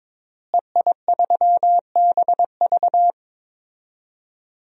Morse code, heard as a keyed tone, about 21 wpm, 700 Hz. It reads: EI3BV
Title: DXCC-100 Call Signs - 5 Characters - Repeated 3x - 22wpm